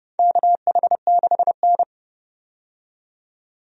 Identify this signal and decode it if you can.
Morse code, heard as a keyed tone, about 30 wpm, 700 Hz. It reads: KH6N
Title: DXCC-100 Call Signs - 4 Characters - 30wpm